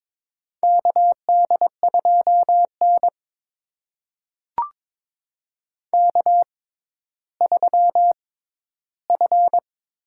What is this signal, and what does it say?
Morse code, heard as a keyed tone, about 22 wpm, spaced as for 13 wpm, 700 Hz. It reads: KD2N K3F
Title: USA Call Signs - 4 Characters - 6x Character Spacing - 22wpm